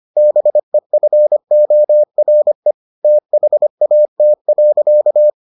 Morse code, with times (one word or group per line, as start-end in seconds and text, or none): 0.2-2.7 BEFORE
3.0-5.3 THAT.